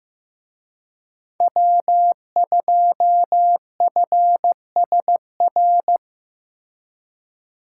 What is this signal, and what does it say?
Morse code, heard as a keyed tone, about 18 wpm, 700 Hz. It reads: W2FSR
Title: Speed Racing - USA Call Signs 15wpm